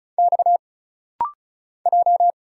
Morse code, heard as a keyed tone, about 36 wpm, 700 Hz. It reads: X J
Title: Single Letters - Rapid-Fire 35wpm